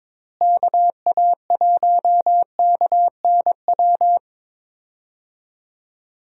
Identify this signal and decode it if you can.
Morse code, heard as a keyed tone, about 21 wpm, 700 Hz. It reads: KA1KNW
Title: USA Call Signs - 6 Characters - Repeated 2x - 22wpm